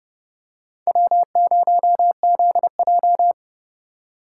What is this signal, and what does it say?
Morse code, heard as a keyed tone, about 30 wpm, 700 Hz. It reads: W0ZJ
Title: USA Call Signs - 4 Characters - 3x Character Spacing - 30wpm